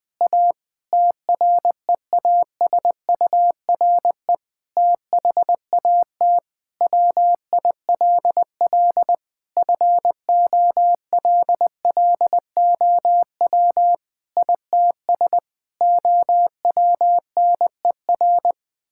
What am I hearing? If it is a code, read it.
Morse code, heard as a keyed tone, about 20 wpm, 700 Hz. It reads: A TREASURE THAT WILL FOLLOW ITS OWNER
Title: Difficult Sentences with Hints - 20wpm